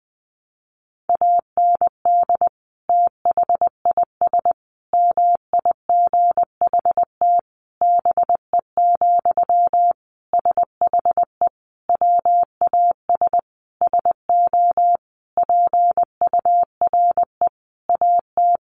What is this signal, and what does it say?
Morse code, heard as a keyed tone, about 20 wpm, 700 Hz. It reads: AND THIS MIGHT BE, SHE WAS SO PURE AT
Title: Sentences from Top 1500 Words 20wpm